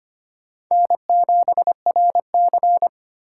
Morse code, heard as a keyed tone, about 25 wpm, 700 Hz. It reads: N7RC
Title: USA Call Signs - 4 Characters - 3x Character Spacing - 25wpm